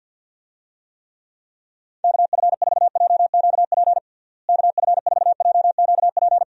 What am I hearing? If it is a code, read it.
Morse code, heard as a keyed tone, about 52 wpm, 700 Hz. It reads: KF4JQP KF4JQP
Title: USA Call Signs - 6 Characters - Repeated 2x - 50wpm